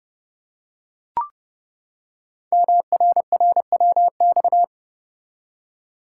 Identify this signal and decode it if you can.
Morse code, heard as a keyed tone, about 30 wpm, 700 Hz. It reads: MRRWX
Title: Code Groups - 5 Letters - 30wpm